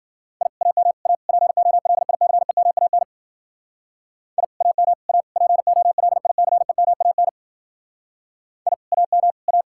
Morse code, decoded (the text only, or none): I AM A POLICEMAN I AM A POLICEMAN I AM A